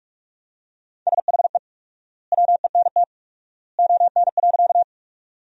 Morse code, heard as a keyed tone, about 45 wpm, 700 Hz. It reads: SHE WENT ON.
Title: 4x Word-Spacing - Sentences from Top 200 Words - 45wpm